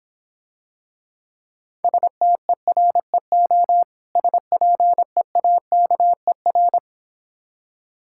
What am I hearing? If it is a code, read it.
Morse code, heard as a keyed tone, about 26 wpm, 690 Hz. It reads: STEREO SPEAKER